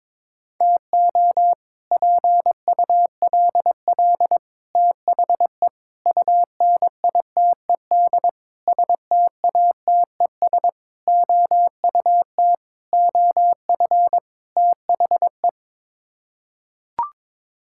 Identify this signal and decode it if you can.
Morse code, heard as a keyed tone, about 22 wpm, 700 Hz. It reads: TO PULL THE UNITED STATES OUT OF THE
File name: practice / 22wpm / Sets of 8 Words - 22wpm